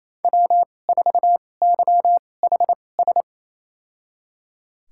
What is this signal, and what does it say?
Morse code, heard as a keyed tone, about 28 wpm, spaced as for 20 wpm, 700 Hz. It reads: W4YHS
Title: USA Call Signs - 2x Character Spacing - 28wpm